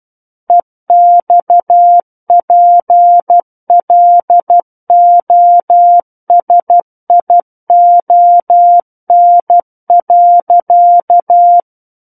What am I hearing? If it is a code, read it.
Morse code, heard as a keyed tone, about 12 wpm, 700 Hz. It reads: EXPLOSION.
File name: Book One / 03-Chapter 3 - 12 WPM